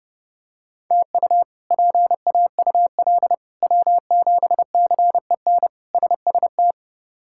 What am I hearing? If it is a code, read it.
Morse code, heard as a keyed tone, about 30 wpm, 700 Hz. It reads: TU PAUL W7CEN SST
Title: K1USN SST QSO Practice - 30wpm